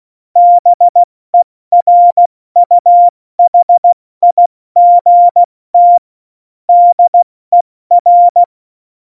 Morse code, heard as a keyed tone, about 16 wpm, 700 Hz. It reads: BERUHIGT DER
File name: Morsekurs / Witze / Witz 26. Dezember 2017, 70 F80